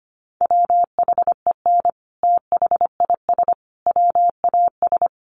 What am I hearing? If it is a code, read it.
Morse code, heard as a keyed tone, about 25 wpm, 700 Hz. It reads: WHEN THIS WAS
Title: Sentences from Top 900 Words 25wpm